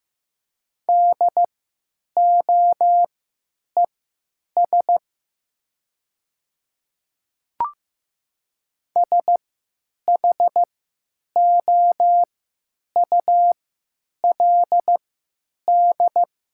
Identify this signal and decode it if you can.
Morse code, heard as a keyed tone, about 15 wpm, spaced as for 9 wpm, 700 Hz. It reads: DOES SHOULD
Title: 3x Character Spacing - Top 300 Words - 15wpm